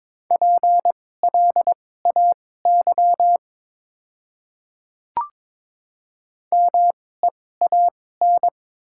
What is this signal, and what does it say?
Morse code, heard as a keyed tone, about 22 wpm, spaced as for 16 wpm, 700 Hz. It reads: PLAY MEAN
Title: 2x Character Spacing - Top 300 Words - 22wpm